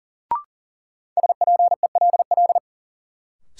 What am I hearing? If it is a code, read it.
Morse code, heard as a keyed tone, about 40 wpm, 700 Hz. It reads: SPELL
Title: Sentences from Top 1000 Words - Review Part 1 - New Word Drill - 40wpm